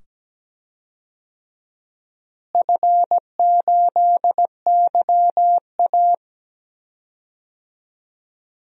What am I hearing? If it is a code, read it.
Morse code, heard as a keyed tone, about 17 wpm, 700 Hz. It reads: F8YA
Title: DXCC-100 Call Signs - 4 Characters - Repeated 2x - 17wpm